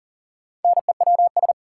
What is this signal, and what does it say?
Morse code, heard as a keyed tone, about 40 wpm, 700 Hz. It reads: NEWS
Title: Sentences from Top 800 Words - New Word Drill - 40wpm